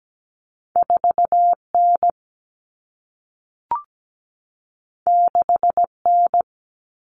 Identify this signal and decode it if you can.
Morse code, heard as a keyed tone, about 17 wpm, 700 Hz. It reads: 4N 6N
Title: Common Cut Numbers - 2-Digit 17wpm